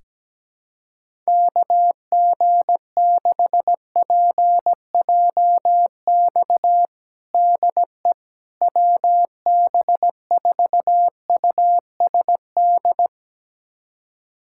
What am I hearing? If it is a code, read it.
Morse code, heard as a keyed tone, about 17 wpm, 700 Hz. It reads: KG6PJX DE WB4USD
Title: USA Callsign DE Callsign - 6 Characters - 17wpm